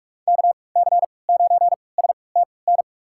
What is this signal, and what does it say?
Morse code, heard as a keyed tone, about 46 wpm, spaced as for 26 wpm, 700 Hz. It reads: KC9STN
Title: USA Call Signs - 3x Character Spacing - 45wpm